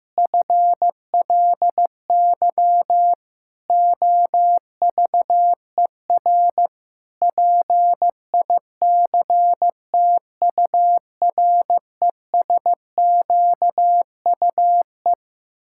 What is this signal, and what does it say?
Morse code, heard as a keyed tone, about 15 wpm, 700 Hz. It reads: FLY OVER PICTURESQUE
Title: Pangram Challenge v2 - 15wpm